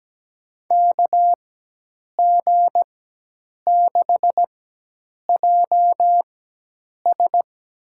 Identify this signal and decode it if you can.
Morse code, heard as a keyed tone, about 17 wpm, spaced as for 8 wpm, 700 Hz. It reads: KG6JS
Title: USA Call Signs - 4x Character Spacing - 17wpm